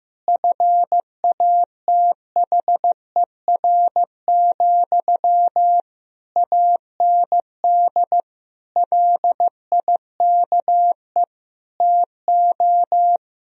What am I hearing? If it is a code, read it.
Morse code, heard as a keyed tone, about 15 wpm, 700 Hz. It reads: FATHER, AND LIKE TO
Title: Sentences from Top 1000 Words - Review Part 1 - 15wpm